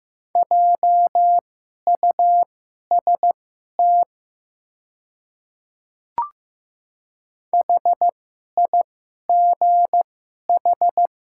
Morse code, heard as a keyed tone, about 15 wpm, spaced as for 11 wpm, 700 Hz. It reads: JUST HIGH
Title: 2x Character Spacing - Top 300 Words - 15wpm